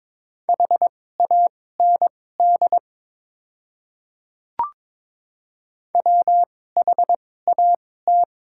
Morse code, heard as a keyed tone, about 22 wpm, spaced as for 16 wpm, 700 Hz. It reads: HAND WHAT
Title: 2x Character Spacing - Top 300 Words - 22wpm